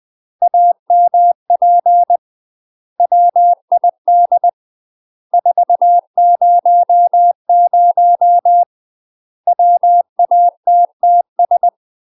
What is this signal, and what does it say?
Morse code, heard as a keyed tone, about 20 wpm, 700 Hz. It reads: AMP WID 400 WATTS
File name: UK MASTER QSOs TRAINING / 20WPM / UK MASTER QSOs 0045